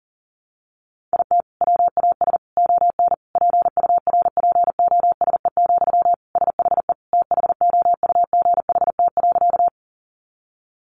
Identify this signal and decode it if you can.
Morse code, heard as a keyed tone, about 40 wpm, 700 Hz. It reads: IT WAS ON PURPOSE, SHE THOUGHT.
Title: Sentences from Top 800 Words 40wpm